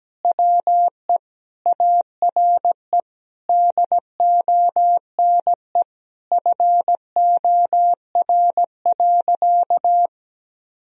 Morse code, heard as a keyed tone, about 17 wpm, 700 Hz. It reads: WE ARE DONE FOR.
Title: Sentences from Top 1000 Words - Review Part 4 - 17wpm